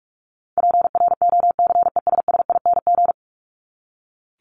Morse code, heard as a keyed tone, about 44 wpm, 700 Hz. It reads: PROCESSING
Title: ING Suffix - Top 100 Words with Base Words 45wpm